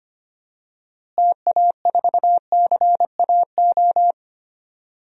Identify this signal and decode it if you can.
Morse code, heard as a keyed tone, about 25 wpm, 700 Hz. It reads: TA4CAO
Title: DXCC-100 Call Signs - 6 Characters - Repeated 2x - 25wpm